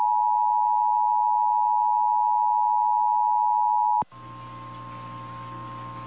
Emergency Alert System (SAME) tones (the 853 and 960 Hz attention tone). It is heard from 0.8 to 5.8 s.
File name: eas / KNCYKBIE / KNCYKBIE